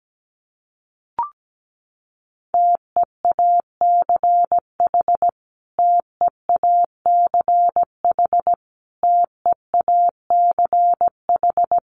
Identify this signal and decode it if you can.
Morse code, heard as a keyed tone, about 17 wpm, 700 Hz. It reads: TEACH TEACH TEACH
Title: Top 500 Words Repeated 17wpm